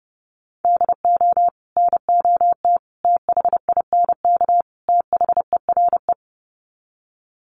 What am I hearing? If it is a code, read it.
Morse code, heard as a keyed tone, about 30 wpm, 700 Hz. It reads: DO NOT THINK THERE
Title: Sets of 4 Words 30wpm